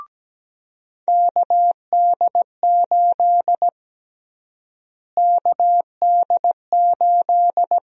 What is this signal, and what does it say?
Morse code, heard as a keyed tone, about 17 wpm, 700 Hz. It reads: KD8 KD8
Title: USA Call Sign Prefixes - Repeated 2x - 17wpm